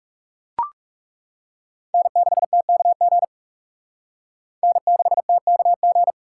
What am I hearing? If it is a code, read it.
Morse code, heard as a keyed tone, about 37 wpm, 700 Hz. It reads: N6TKG N6TKG
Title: Speed Racing - USA 1x3 Call Signs 30wpm